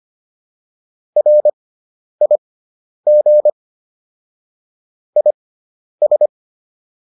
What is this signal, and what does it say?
Morse code, heard as a keyed tone, about 25 wpm, spaced as for 10 wpm, 600 Hz. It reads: RIG IS